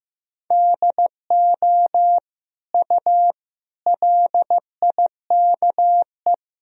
Morse code, heard as a keyed tone, about 15 wpm, 700 Hz. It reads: DO U LIKE
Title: Sentences from Top 100 Words and Abbreviations 15wpm